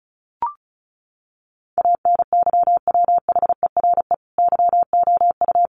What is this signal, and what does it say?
Morse code, heard as a keyed tone, about 34 wpm, 700 Hz. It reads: ANYWHERE YOU
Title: Sentences from Top 1400 Words 35wpm